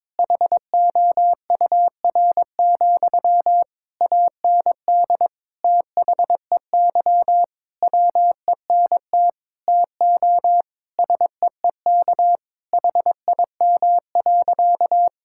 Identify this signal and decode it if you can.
Morse code, heard as a keyed tone, about 22 wpm, 700 Hz. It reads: HOUR, AND THEY WENT TO SEEK HIM.
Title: Sentences from Top 1600 Words 22wpm